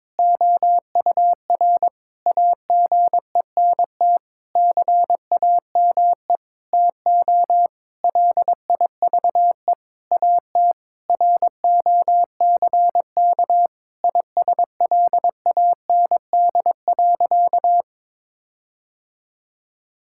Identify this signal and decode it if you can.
Morse code, heard as a keyed tone, about 22 wpm, 700 Hz. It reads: OUR AGENT CAME TO LIVE AT ROCK ISLAND.
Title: Sentences from Top 1600 Words 22wpm